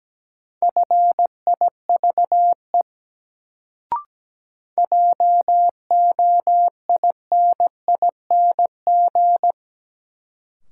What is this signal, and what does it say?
Morse code, heard as a keyed tone, about 17 wpm, 700 Hz. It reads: FIVE JOINING
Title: Sets of 5 Words - New Word Drill - 17wpm